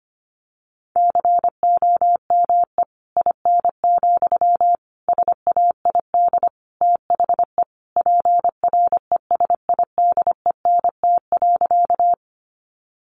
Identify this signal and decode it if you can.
Morse code, heard as a keyed tone, about 25 wpm, 700 Hz. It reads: COME IN, SAID THE PRESIDENT.
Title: Sentences from Top 1000 Words 25wpm